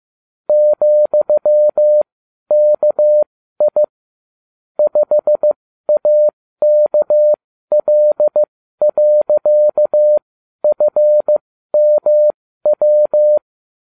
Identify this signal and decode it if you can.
Morse code, heard as a keyed tone, about 15 wpm, spaced as for 12 wpm, 600 Hz. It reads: ,KI 5AKL.FMW